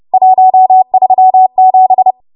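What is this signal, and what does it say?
Morse code, heard as a keyed tone, about 30 wpm, 750 Hz. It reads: 137
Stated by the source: Morse practice transmission at 30 WPM.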